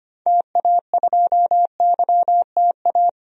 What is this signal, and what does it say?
Morse code, heard as a keyed tone, about 25 wpm, 700 Hz. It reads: TA2YTA
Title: DXCC-100 Call Signs - 6 Characters - 25wpm